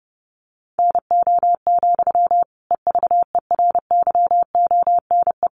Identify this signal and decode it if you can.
Morse code, heard as a keyed tone, about 30 wpm, 700 Hz. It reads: NO, EVERYONE
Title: Sentences from Top 800 Words 30wpm